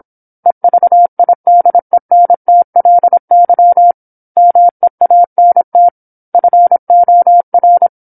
Morse code, heard as a keyed tone, about 25 wpm, 700 Hz. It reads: EVIDENTLY MEANT FOR
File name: 08-Chapter-08 26 WPM